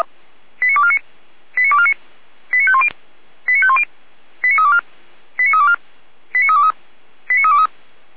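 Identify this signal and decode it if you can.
Five-tone selective call (ZVEI), heard as quick runs of five tones, tones of about 70 ms. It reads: 89249 89249 89510 89510 89324 89324 89323 89323
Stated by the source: ZVEI five-tone call received by SDR, server of Feuerwehr Orlamünde fire brigade.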